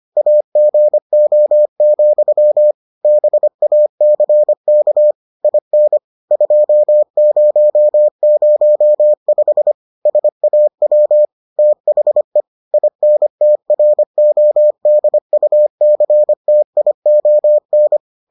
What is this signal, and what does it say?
Morse code, heard as a keyed tone, about 25 wpm, 600 Hz. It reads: AGO, BACK IN 2005 SAW THE INTRODUCTION